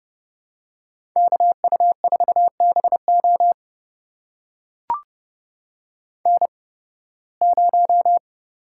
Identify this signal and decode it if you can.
Morse code, heard as a keyed tone, about 31 wpm, spaced as for 20 wpm, 700 Hz. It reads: KU4BO N0
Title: USA Call Signs - 5 Characters - 8x Character Spacing - 30wpm